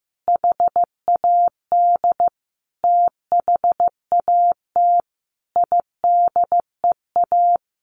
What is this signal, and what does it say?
Morse code, heard as a keyed tone, about 15 wpm, 700 Hz. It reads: HAD THAT IDEA
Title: Sentences from Top 1400 Words 15wpm